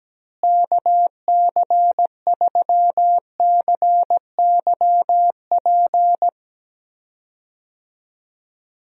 Morse code, heard as a keyed tone, about 17 wpm, 700 Hz. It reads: KC3CYP
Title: USA Call Signs - 6 Characters - Repeated 3x - 17wpm